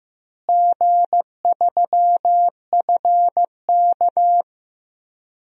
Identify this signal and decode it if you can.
Morse code, heard as a keyed tone, about 16 wpm, 700 Hz. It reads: G3FK
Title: DXCC-100 Call Signs - 4 Characters - Repeated 2x - 15wpm